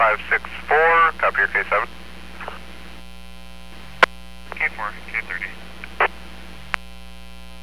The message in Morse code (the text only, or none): none